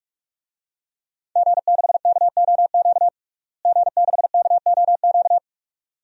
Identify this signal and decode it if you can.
Morse code, heard as a keyed tone, about 46 wpm, 700 Hz. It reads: G6KOQ G6KOQ
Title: DXCC-100 Call Signs - 5 Characters - Repeated 2x - 45wpm